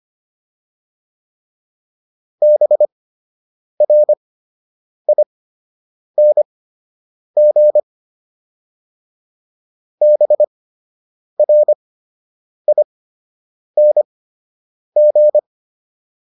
Morse code, heard as a keyed tone, about 25 wpm, spaced as for 8 wpm, 600 Hz. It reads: BRING BRING